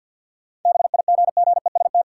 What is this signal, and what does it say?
Morse code, heard as a keyed tone, about 49 wpm, 700 Hz. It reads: BIGGEST
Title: Sets of 4 Words - New Word Drill - 50wpm